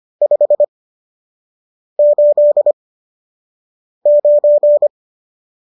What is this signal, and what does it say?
Morse code, heard as a keyed tone, about 25 wpm, spaced as for 6 wpm, 590 Hz. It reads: 589
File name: qso003 06